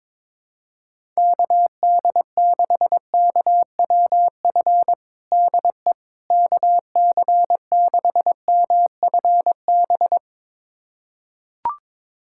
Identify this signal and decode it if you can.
Morse code, heard as a keyed tone, about 22 wpm, 700 Hz. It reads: KD6KWF DE KC6MFB
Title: USA Callsign DE Callsign - 6 Characters - 22wpm